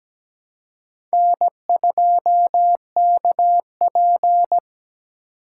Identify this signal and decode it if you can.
Morse code, heard as a keyed tone, about 20 wpm, 700 Hz. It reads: N2KP
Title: Speed Racing - USA 1x2 Call Signs 17wpm